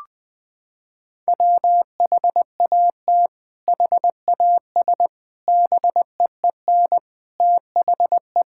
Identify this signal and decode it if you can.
Morse code, heard as a keyed tone, about 20 wpm, 700 Hz. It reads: WHAT HAS BEEN THE
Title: Sentences from Top 1000 Words - Review Part 6 - 20wpm